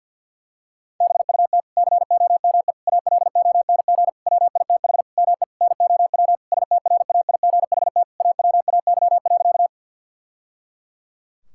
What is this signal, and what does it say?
Morse code, heard as a keyed tone, about 50 wpm, 700 Hz. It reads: BUT COME ALONG WITH ME NOW STRAIGHT AWAY.
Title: Sentences from Top 1000 Words - Review Part 6 - 50wpm